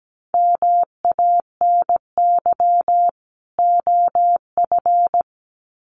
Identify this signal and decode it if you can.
Morse code, heard as a keyed tone, about 17 wpm, 700 Hz. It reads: MANY OF
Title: Sets of 2 Words 17wpm